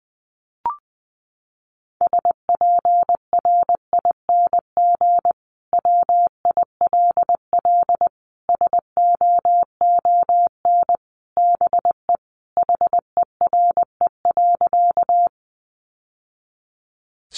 Morse code, heard as a keyed tone, about 20 wpm, 700 Hz. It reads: SPRING WILL SOON BE HERE.
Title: Sentences from Top 1200 Words 20wpm